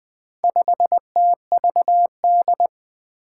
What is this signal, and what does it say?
Morse code, heard as a keyed tone, about 20 wpm, 700 Hz. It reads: 5TVD
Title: Four Challenging Letter-Number Combinations 20wpm